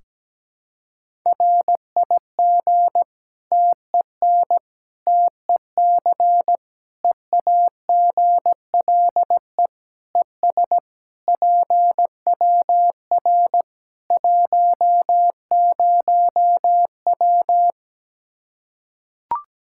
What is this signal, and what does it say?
Morse code, heard as a keyed tone, about 17 wpm, 700 Hz. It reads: RIG TEN TEC EAGLE ES PWR 10W
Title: K1USN SST QSO Practice 17wpm